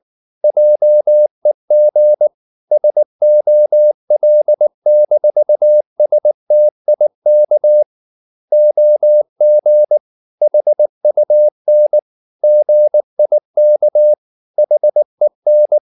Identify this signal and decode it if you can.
Morse code, heard as a keyed tone, about 19 wpm, 600 Hz. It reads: JEG SOL-STIK OG HUN GIK HEN
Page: Ferie 19 wpm